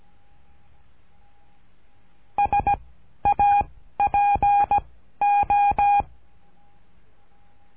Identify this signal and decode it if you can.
Morse code, heard as a keyed tone, about 17 wpm, 810 Hz. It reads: S A P O
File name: cursos / CURSO DE TELEGRAFIA / Telegrafia - Trilha 13